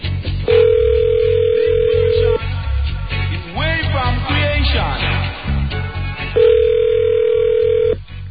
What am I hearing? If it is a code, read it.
Ringback tone (North America), 440 + 480 Hz.